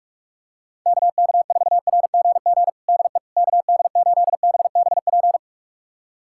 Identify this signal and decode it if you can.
Morse code, heard as a keyed tone, about 43 wpm, 700 Hz. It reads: KK4RGG DE KD8BBP
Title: USA Callsign DE Callsign 45wpm